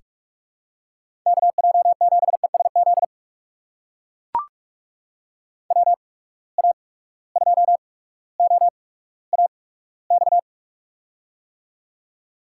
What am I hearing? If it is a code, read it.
Morse code, heard as a keyed tone, about 41 wpm, spaced as for 19 wpm, 700 Hz. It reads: KJ7ESZ WA2OAX
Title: USA Call Signs - 6 Characters - 8x Character Spacing - 45wpm